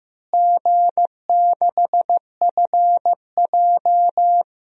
Morse code, heard as a keyed tone, about 15 wpm, 700 Hz. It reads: G6FJ
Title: DXCC-100 Call Signs - 4 Characters - Repeated 3x - 15wpm